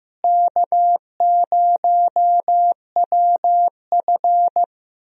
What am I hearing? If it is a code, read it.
Morse code, heard as a keyed tone, about 15 wpm, 700 Hz. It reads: K0WF
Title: USA Call Signs - 4 Characters - Repeated 3x - 15wpm